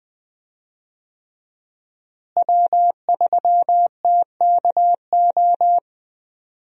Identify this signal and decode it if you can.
Morse code, heard as a keyed tone, about 20 wpm, 700 Hz. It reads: W3TKO
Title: USA Call Signs - 5 Characters - Repeated 3x - 20wpm